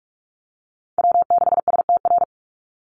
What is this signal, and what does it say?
Morse code, heard as a keyed tone, about 44 wpm, 700 Hz. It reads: W6STR
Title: NA SOTA Chaser Callsigns 45wpm